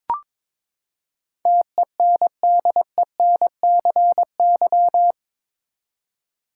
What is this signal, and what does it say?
Morse code, heard as a keyed tone, about 22 wpm, 700 Hz. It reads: TENDENCY
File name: practice / 22wpm / Eight Letter Words - 22wpm